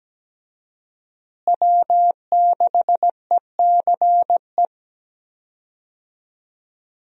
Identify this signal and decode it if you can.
Morse code, heard as a keyed tone, about 17 wpm, 700 Hz. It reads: W6ECE
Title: USA Call Signs - 5 Characters - Repeated 2x - 17wpm